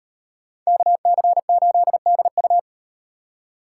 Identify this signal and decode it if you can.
Morse code, heard as a keyed tone, about 38 wpm, 700 Hz. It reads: KC8DU